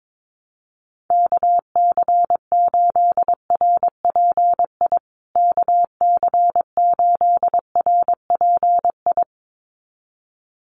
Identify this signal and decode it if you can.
Morse code, heard as a keyed tone, about 22 wpm, 700 Hz. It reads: KC8RPI KC8RPI
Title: USA Call Signs Repeated 2x 22wpm